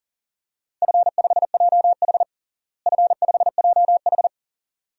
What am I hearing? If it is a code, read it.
Morse code, heard as a keyed tone, about 37 wpm, 700 Hz. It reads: F5JH F5JH
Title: DXCC-100 Call Signs - 4 Characters - Repeated 2x - 40wpm